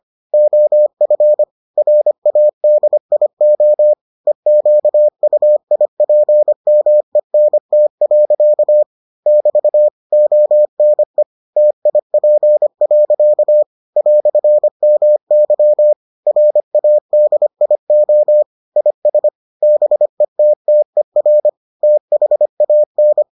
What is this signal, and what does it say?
Morse code, heard as a keyed tone, about 25 wpm, 600 Hz. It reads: OF RADIO EQUIPMENT. <BT> ONE TIP. "MY RADIO IS BETTER THAN